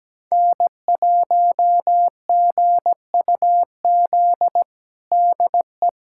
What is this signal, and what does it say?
Morse code, heard as a keyed tone, about 17 wpm, 700 Hz. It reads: N1GUZ DE
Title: USA Callsign DE Callsign 17wpm